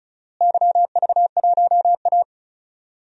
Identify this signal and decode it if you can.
Morse code, heard as a keyed tone, about 36 wpm, 700 Hz. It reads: YV1A
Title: DXCC-100 Call Signs - 4 Characters - Repeated 2x - 35wpm